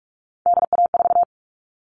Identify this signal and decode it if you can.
Morse code, heard as a keyed tone, about 44 wpm, 700 Hz. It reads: DA4
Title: All Cut Numbers - 3-Digit 45wpm